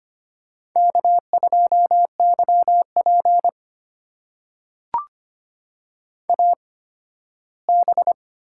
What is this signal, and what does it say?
Morse code, heard as a keyed tone, about 25 wpm, spaced as for 15 wpm, 700 Hz. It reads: K2YP AB
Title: USA Call Signs - 8x Character Spacing - 25wpm